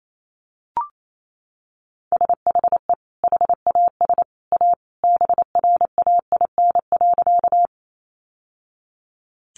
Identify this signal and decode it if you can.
Morse code, heard as a keyed tone, about 28 wpm, 700 Hz. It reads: SHE HAS A BRAIN.
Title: Sentences from Top 1100 Words 28wpm